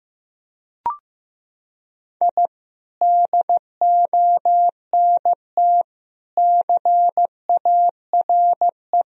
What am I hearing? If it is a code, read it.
Morse code, heard as a keyed tone, about 16 wpm, 700 Hz. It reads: I DONT CARE
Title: Sentences from Top 1000 Words - Review Part 1 - 15wpm